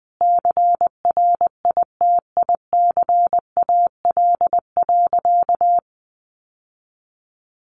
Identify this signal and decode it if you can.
Morse code, heard as a keyed tone, about 20 wpm, 700 Hz. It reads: CRITICAL.
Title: Sentences from Top 1100 Words 20wpm